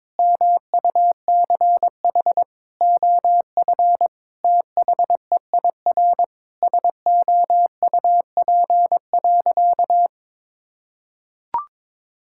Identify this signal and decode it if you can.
Morse code, heard as a keyed tone, about 22 wpm, 700 Hz. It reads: MUCH OF THEIR SOUP.
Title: Sentences from Top 1000 Words - Review Part 6 - 22wpm